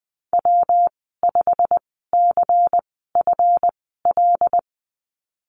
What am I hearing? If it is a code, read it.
Morse code, heard as a keyed tone, about 20 wpm, spaced as for 15 wpm, 700 Hz. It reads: W5CFL
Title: 140 Lesson 23g - Letter f - Call Signs 20wpm